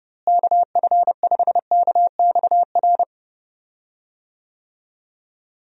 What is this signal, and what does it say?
Morse code, heard as a keyed tone, about 31 wpm, 700 Hz. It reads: KF5KXR
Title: USA Call Signs - 6 Characters - Repeated 3x - 30wpm